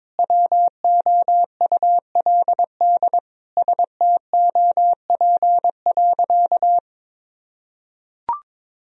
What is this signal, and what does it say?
Morse code, heard as a keyed tone, about 21 wpm, 700 Hz. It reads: WOULD STOP.
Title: Sentences from Top 1000 Words - Review Part 1 - 22wpm